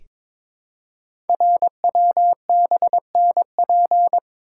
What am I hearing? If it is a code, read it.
Morse code, heard as a keyed tone, about 22 wpm, 700 Hz. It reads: RWBNP
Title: Code Groups - 5 Letters - 22wpm